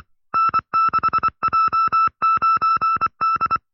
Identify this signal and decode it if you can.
Morse code, heard as a keyed tone, about 25 wpm, 1310 Hz. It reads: N6J9D
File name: FreeFiles / CW-practice / 20WPM-side2